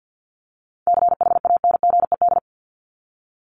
Morse code, heard as a keyed tone, about 50 wpm, 700 Hz. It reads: CHANGED
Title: ED Suffix - Top 100 Words with Base Words 50wpm